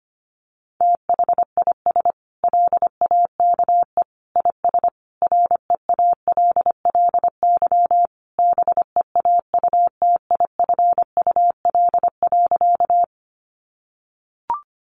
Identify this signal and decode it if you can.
Morse code, heard as a keyed tone, about 25 wpm, 700 Hz. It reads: THIS LAKE IS REALLY BEAUTIFUL.
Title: Sentences from Top 1100 Words 25wpm